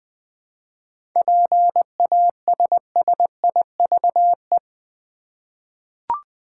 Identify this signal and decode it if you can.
Morse code, heard as a keyed tone, about 20 wpm, 700 Hz. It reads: PASSIVE
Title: 7 Letter Words - First 3 Letters Given - 20wpm